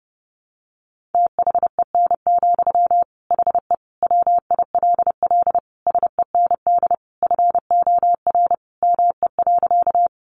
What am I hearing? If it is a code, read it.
Morse code, heard as a keyed tone, about 30 wpm, 700 Hz. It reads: THEN, HE WILL SEND FOR ME.
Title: Sentences from Top 1500 Words 30wpm